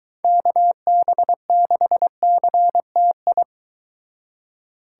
Morse code, heard as a keyed tone, about 23 wpm, 700 Hz. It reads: KB6CTI